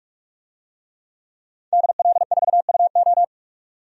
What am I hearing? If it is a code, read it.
Morse code, heard as a keyed tone, about 42 wpm, 700 Hz. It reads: DL4UO
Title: DXCC-100 Call Signs - 5 Characters - 45wpm